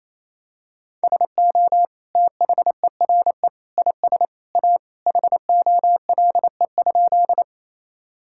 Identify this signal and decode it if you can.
Morse code, heard as a keyed tone, about 28 wpm, 700 Hz. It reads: SO THERE IS A HOLE?
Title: Sentences from Top 1600 Words 28wpm